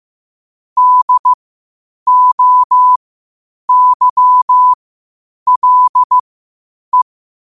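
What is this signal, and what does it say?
Morse code, heard as a keyed tone, about 15 wpm, spaced as for 8 wpm, 1000 Hz. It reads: DOYLE
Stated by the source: Morse code quotes at 8 words per minute from QOTD at 8 WPM.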